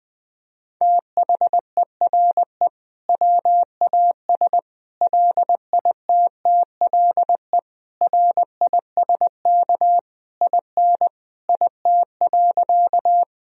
Morse code, heard as a keyed tone, about 20 wpm, 700 Hz. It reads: THERE WAS LITTLE RISK IN IT.
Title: Sentences from Top 1000 Words - Review Part 4 - 20wpm